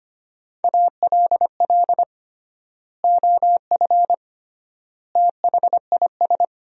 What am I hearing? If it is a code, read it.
Morse code, heard as a keyed tone, about 25 wpm, 700 Hz. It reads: ALL OF THIS
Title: 3x Word-Spacing - Sentences from Top 500 Words - 25wpm